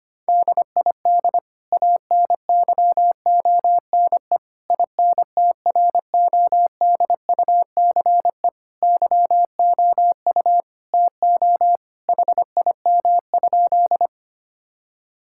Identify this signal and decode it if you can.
Morse code, heard as a keyed tone, about 25 wpm, 700 Hz. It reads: DID ANYONE INTRODUCE YOU TO HIM?
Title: Sentences from Top 1600 Words 25wpm